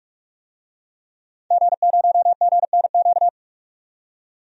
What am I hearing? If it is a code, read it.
Morse code, heard as a keyed tone, about 44 wpm, 700 Hz. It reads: G0GNQ
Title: DXCC-100 Call Signs - 5 Characters - 45wpm